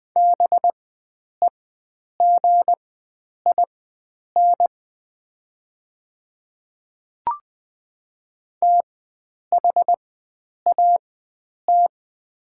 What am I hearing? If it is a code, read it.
Morse code, heard as a keyed tone, about 20 wpm, spaced as for 9 wpm, 700 Hz. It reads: BEGIN THAT
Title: 4x Character Spacing - Top 300 Words - 20wpm